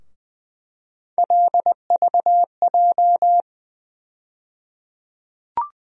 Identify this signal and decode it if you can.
Morse code, heard as a keyed tone, about 20 wpm, 700 Hz. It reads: LVJ
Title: USA Call Sign Postfixes - 3 Letters 20wpm